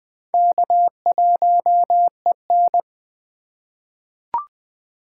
Morse code, heard as a keyed tone, about 20 wpm, 700 Hz. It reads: K1EN
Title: USA Call Signs - 4 Characters - 3x Character Spacing - 20wpm